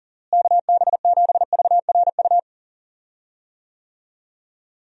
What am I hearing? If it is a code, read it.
Morse code, heard as a keyed tone, about 40 wpm, 700 Hz. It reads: KB7VRU
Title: USA Call Signs - 6 Characters - Repeated 2x - 40wpm